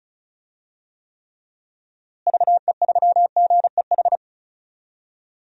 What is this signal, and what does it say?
Morse code, heard as a keyed tone, about 36 wpm, 700 Hz. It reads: VE3GEH
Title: DXCC-100 Call Signs - 6 Characters - Repeated 3x - 35wpm